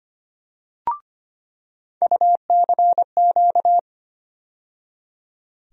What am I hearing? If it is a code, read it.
Morse code, heard as a keyed tone, about 25 wpm, 700 Hz. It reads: UCQ
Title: USA Call Sign Postfixes - 3 Letters 25wpm